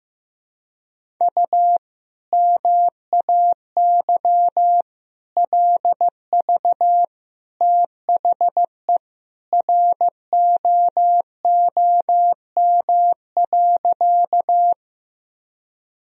Morse code, heard as a keyed tone, about 15 wpm, 700 Hz. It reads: U MAY LV THE ROOM.
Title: Sentences from Top 400 Words and Abbreviations 15wpm